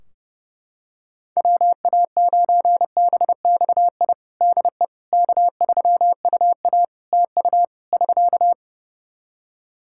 Morse code, heard as a keyed tone, about 32 wpm, 700 Hz. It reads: WA9BXI DE K3UA TU <SK>
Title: K1USN SST QSO Practice 30wpm